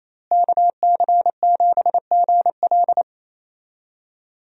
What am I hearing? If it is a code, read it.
Morse code, heard as a keyed tone, about 28 wpm, 700 Hz. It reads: KC7GL
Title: USA Call Signs - 5 Characters - Repeated 2x - 28wpm